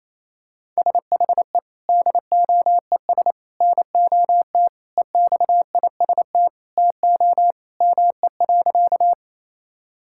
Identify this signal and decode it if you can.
Morse code, heard as a keyed tone, about 28 wpm, 700 Hz. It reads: SHE DOES NOT EXIST TO ME.
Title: Sentences from Top 1000 Words - Review Part 5 - 28wpm